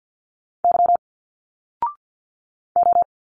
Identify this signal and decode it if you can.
Morse code, heard as a keyed tone, about 52 wpm, 700 Hz. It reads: Y O
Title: Single Letters 50wpm